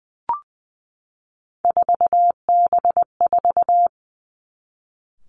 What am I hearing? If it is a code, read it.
Morse code, heard as a keyed tone, about 20 wpm, 700 Hz. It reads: 4B4
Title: Three Challenging Letter-Number Combinations 20wpm